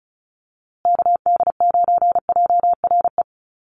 Morse code, heard as a keyed tone, about 34 wpm, 700 Hz. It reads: KD9JRE
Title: USA Call Signs Repeated 2x 35wpm